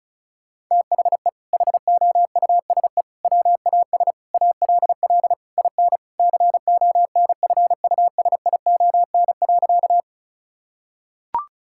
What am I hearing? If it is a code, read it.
Morse code, heard as a keyed tone, about 34 wpm, 700 Hz. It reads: THE HOUSE WAS ALL IN CONFUSION.
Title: Sentences from Top 1600 Words 35wpm